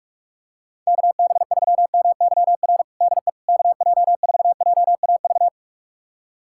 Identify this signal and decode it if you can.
Morse code, heard as a keyed tone, about 46 wpm, 700 Hz. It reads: KB3MYR DE KJ4JAV
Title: USA Callsign DE Callsign - 6 Characters - 45wpm